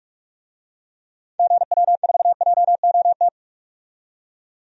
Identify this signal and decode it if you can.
Morse code, heard as a keyed tone, about 47 wpm, 700 Hz. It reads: GW4JOT
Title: DXCC-100 Call Signs - 6 Characters - 45wpm